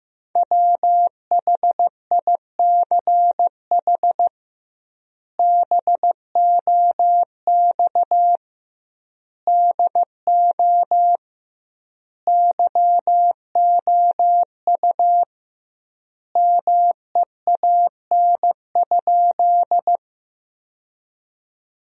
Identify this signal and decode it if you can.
Morse code, heard as a keyed tone, about 15 wpm, 700 Hz. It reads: WHICH BOX DO YOU MEAN?
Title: 2x Word-Spacing - Sentences from Top 500 Words - 15wpm